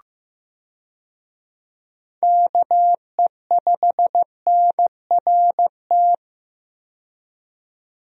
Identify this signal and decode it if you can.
Morse code, heard as a keyed tone, about 15 wpm, 700 Hz. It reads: KE5NRT
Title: USA Call Signs - 6 Characters - 15wpm